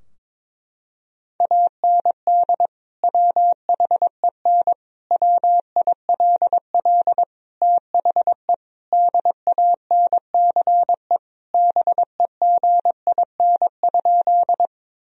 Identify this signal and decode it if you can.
Morse code, heard as a keyed tone, about 22 wpm, 700 Hz. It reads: AND WHEN WILL THE DANCE BEGIN?
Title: Sentences from Top 1000 Words - Review Part 4 - 22wpm